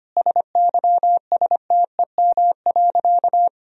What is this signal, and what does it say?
Morse code, heard as a keyed tone, about 25 wpm, 700 Hz. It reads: SYSTEM.
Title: Sentences from Top 1000 Words - Review Part 3 - 25wpm